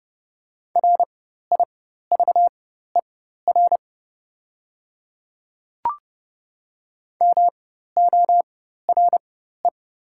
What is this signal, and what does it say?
Morse code, heard as a keyed tone, about 29 wpm, spaced as for 14 wpm, 700 Hz. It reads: RIVER MORE
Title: 4x Character Spacing - Top 300 Words - 30wpm